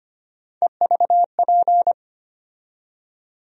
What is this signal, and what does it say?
Morse code, heard as a keyed tone, about 25 wpm, 700 Hz. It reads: EVP
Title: USA Call Sign Postfixes - 3 Letters 25wpm